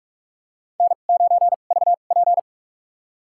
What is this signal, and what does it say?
Morse code, heard as a keyed tone, about 46 wpm, spaced as for 32 wpm, 700 Hz. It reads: N9VP
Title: USA Call Signs - 4 Characters - 2x Character Spacing - 45wpm